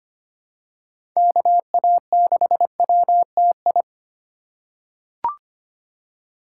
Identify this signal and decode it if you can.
Morse code, heard as a keyed tone, about 25 wpm, 700 Hz. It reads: KA6WTI
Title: USA Call Signs - 6 Characters - 3x Character Spacing - 25wpm